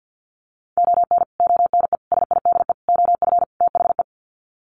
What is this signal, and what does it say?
Morse code, heard as a keyed tone, about 50 wpm, 700 Hz. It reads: ON ONE SIDE OF THE
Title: Sets of 5 Words 50wpm